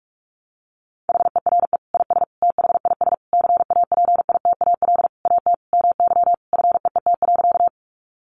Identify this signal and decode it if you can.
Morse code, heard as a keyed tone, about 44 wpm, 700 Hz. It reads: HERE IS THIS CAPITAL AT MY FEET.
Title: Sentences from Top 1000 Words 45wpm